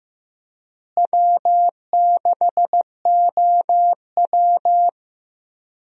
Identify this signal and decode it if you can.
Morse code, heard as a keyed tone, about 15 wpm, 700 Hz. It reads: W6OW
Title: USA Call Signs - 4 Characters - Repeated 3x - 15wpm